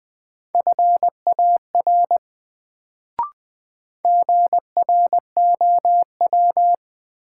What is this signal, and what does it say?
Morse code, heard as a keyed tone, about 20 wpm, 700 Hz. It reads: FAR GROW
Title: Sentences from Top 300 Words - New Word Drill - 20wpm